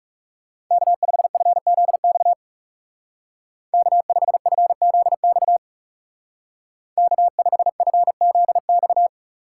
Morse code, heard as a keyed tone, about 35 wpm, 700 Hz. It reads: K5FZX K5FZX K5FZX
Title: Speed Racing - USA 1x3 Call Signs 30wpm